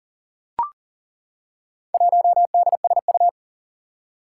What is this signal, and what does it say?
Morse code, heard as a keyed tone, about 42 wpm, 700 Hz. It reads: 1DSU
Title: Four Challenging Letter-Number Combinations 40wpm